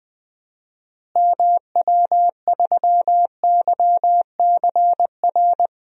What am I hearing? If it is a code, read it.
Morse code, heard as a keyed tone, about 20 wpm, 700 Hz. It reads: MW3YCR
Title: DXCC-100 Call Signs - 6 Characters - Repeated 3x - 20wpm